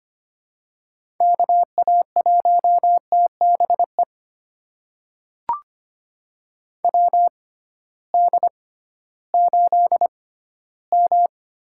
Morse code, heard as a keyed tone, about 25 wpm, spaced as for 14 wpm, 700 Hz. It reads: KA1TBE WD8M
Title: USA Call Signs - 6 Characters - 6x Character Spacing - 25wpm